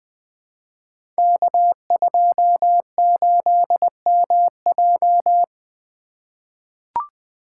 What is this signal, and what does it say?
Morse code, heard as a keyed tone, about 20 wpm, 700 Hz. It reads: K28MJ
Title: Code Groups - 5 Letter:Numbers - 20wpm